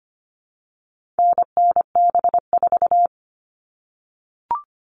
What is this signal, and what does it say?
Morse code, heard as a keyed tone, about 25 wpm, 700 Hz. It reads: NNB4
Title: All Cut Numbers - 4-Digit 25wpm